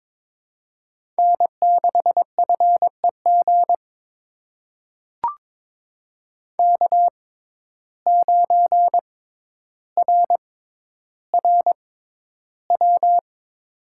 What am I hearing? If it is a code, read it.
Morse code, heard as a keyed tone, about 22 wpm, spaced as for 11 wpm, 700 Hz. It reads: N6FEG K9RRW
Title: USA Call Signs - 5 Characters - 6x Character Spacing - 22wpm